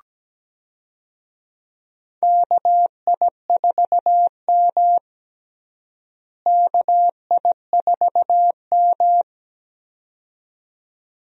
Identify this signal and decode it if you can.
Morse code, heard as a keyed tone, about 17 wpm, 700 Hz. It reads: KI4M KI4M
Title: USA Call Signs - 4 Characters - Repeated 2x - 17wpm